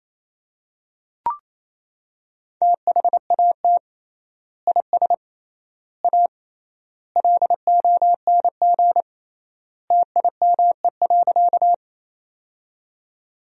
Morse code, heard as a keyed tone, about 28 wpm, 700 Hz. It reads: THAT IS A LONG TIME.
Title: 3x Word-Spacing - Sentences from Top 100 Words - 28wpm